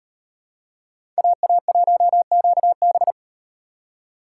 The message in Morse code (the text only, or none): AA1QB